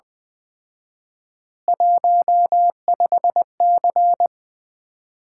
Morse code, heard as a keyed tone, about 20 wpm, 700 Hz. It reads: 15C
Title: Common QSO Phrases - 4x Word-Spacing - 20wpm